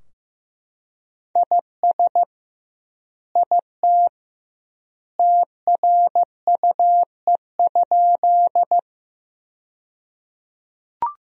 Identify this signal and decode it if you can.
Morse code, heard as a keyed tone, about 15 wpm, 700 Hz. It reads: IS IT TRUE?
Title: 2x Word-Spacing - Sentences from Top 400 Words - 15wpm